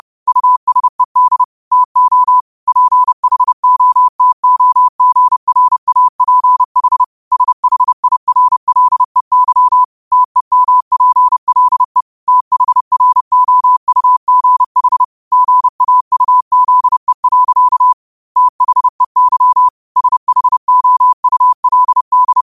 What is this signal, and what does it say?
Morse code, heard as a keyed tone, about 30 wpm, 1000 Hz. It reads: USED TO PHOTOGRAPH SHIRLEY TEMPLE THROUGH GAUZE. THEY SHOULD
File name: qotdfiles / 30wpm